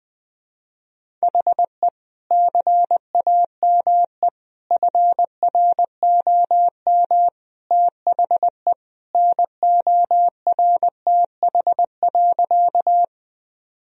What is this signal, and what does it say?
Morse code, heard as a keyed tone, about 20 wpm, 700 Hz. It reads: HE CAME FROM THE NORTH.
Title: Sentences from Top 1000 Words - Review Part 1 - 20wpm